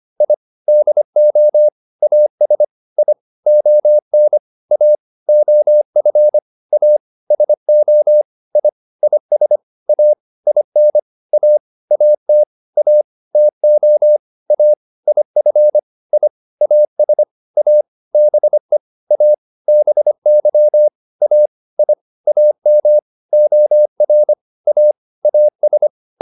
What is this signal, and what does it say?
Morse code, heard as a keyed tone, about 25 wpm, 600 Hz. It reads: I DO AS I ON A OF A SO I IS A IN A AT A TO A IF I AS A BE A BY A I AM OR A AS